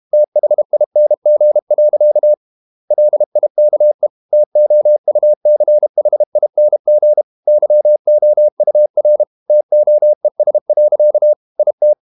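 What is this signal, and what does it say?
Morse code, heard as a keyed tone, about 32 wpm, 600 Hz. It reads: THING. LIKE TOUCHING YOUR TOES. IT
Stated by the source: Morse code practice at 32 wpm from CW med Gnister, reading Ken Follett's novel Never in English.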